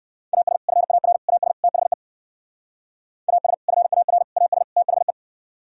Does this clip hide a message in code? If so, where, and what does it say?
Yes: Morse code from 0.3 to 5.1 s, heard as a keyed tone, about 59 wpm, 700 Hz. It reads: AS FAR AS THE AS FAR AS THE